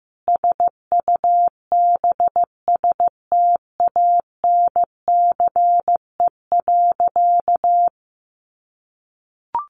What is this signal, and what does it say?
Morse code, heard as a keyed tone, about 15 wpm, 700 Hz. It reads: SUBSTANCE.
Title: Sentences from Top 1500 Words 15wpm